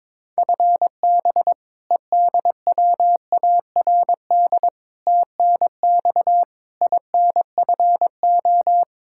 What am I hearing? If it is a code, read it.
Morse code, heard as a keyed tone, about 22 wpm, 700 Hz. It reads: FB EDWARD TNX INFO
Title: Practice QSO - Exchange 3 - More Info - 22wpm